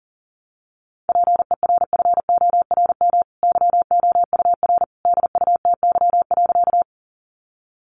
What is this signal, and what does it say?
Morse code, heard as a keyed tone, about 40 wpm, 700 Hz. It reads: PERFORM YOUR DUTY.
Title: Sentences from Top 1500 Words 40wpm